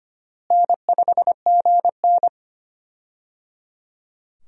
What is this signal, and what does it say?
Morse code, heard as a keyed tone, about 25 wpm, 700 Hz. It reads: N5GN
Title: USA Call Signs - 4 Characters - Repeated 3x - 25wpm